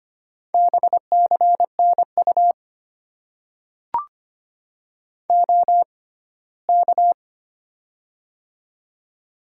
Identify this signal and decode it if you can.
Morse code, heard as a keyed tone, about 25 wpm, spaced as for 17 wpm, 700 Hz. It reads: BCNU OK
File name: Top 90 QSO Elements - 6x Character Spacing - 25wpm